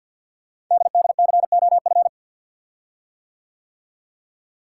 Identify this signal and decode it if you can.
Morse code, heard as a keyed tone, about 52 wpm, 700 Hz. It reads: DDCOF
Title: Code Groups - 5 Letters - 50wpm